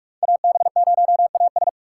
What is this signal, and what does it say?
Morse code, heard as a keyed tone, about 42 wpm, 700 Hz. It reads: AB0AS
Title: USA Call Signs - 5 Characters - 3x Character Spacing - 45wpm